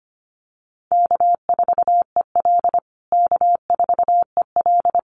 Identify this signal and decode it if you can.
Morse code, heard as a keyed tone, about 25 wpm, 700 Hz. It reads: K4EL K4EL
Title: USA Call Signs Repeated 3x 25wpm